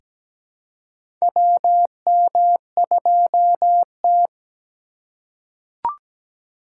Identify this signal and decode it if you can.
Morse code, heard as a keyed tone, about 17 wpm, 700 Hz. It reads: WM2T
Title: USA Call Signs - 4 Characters - 4x Character Spacing - 17wpm